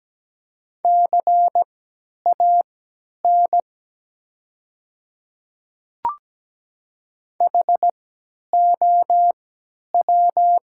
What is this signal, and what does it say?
Morse code, heard as a keyed tone, about 17 wpm, spaced as for 10 wpm, 700 Hz. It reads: CAN HOW
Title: 3x Character Spacing - Top 100 Words - 17wpm